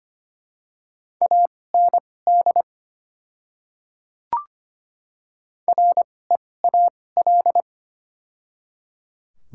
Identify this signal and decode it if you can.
Morse code, heard as a keyed tone, about 25 wpm, spaced as for 18 wpm, 700 Hz. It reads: AND REAL
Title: 2x Character Spacing - Top 300 Words - 25wpm